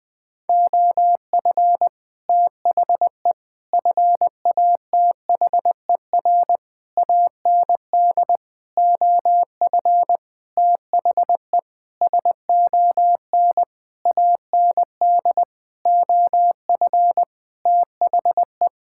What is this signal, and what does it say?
Morse code, heard as a keyed tone, about 20 wpm, 700 Hz. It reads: OF THE FATHER AND OF THE SON AND OF THE
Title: Sentences with 9-Word Phrases - 20wpm